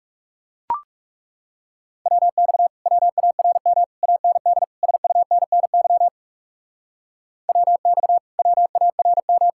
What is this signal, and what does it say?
Morse code, heard as a keyed tone, about 38 wpm, 700 Hz. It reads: WX WARM AND SUNNY WX WARM